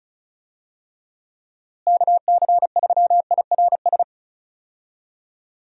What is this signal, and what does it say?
Morse code, heard as a keyed tone, about 36 wpm, 700 Hz. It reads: KC3IRS
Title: USA Call Signs - 6 Characters - Repeated 3x - 35wpm